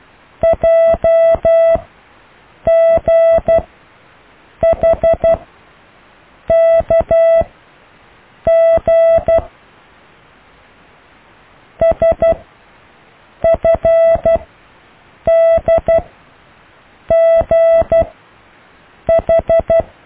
Morse code, heard as a keyed tone, about 12 wpm, spaced as for 6 wpm, 660 Hz. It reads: JGHKG SFDGH
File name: cw / Alle letters